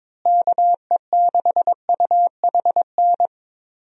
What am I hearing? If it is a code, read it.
Morse code, heard as a keyed tone, about 22 wpm, 700 Hz. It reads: KE6UHN
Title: USA Call Signs - 6 Characters - Repeated 3x - 22wpm